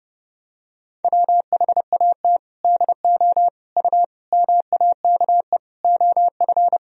Morse code, heard as a keyed tone, about 32 wpm, 700 Hz. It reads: WHAT DO U MAKE OF
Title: Sentences from Top 100 Words and Abbreviations 30wpm